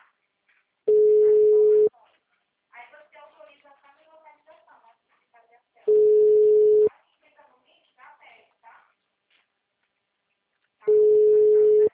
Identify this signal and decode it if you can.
Ringback tone (Europe, 425 Hz), with speech.
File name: out- -2023- - - .99105